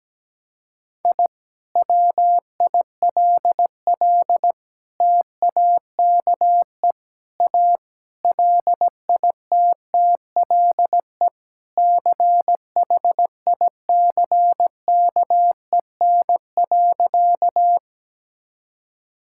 Morse code, heard as a keyed tone, about 17 wpm, 700 Hz. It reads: I WILL TAKE A LITTLE CHICKEN.
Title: Sentences from Top 1000 Words - Review Part 5 - 17wpm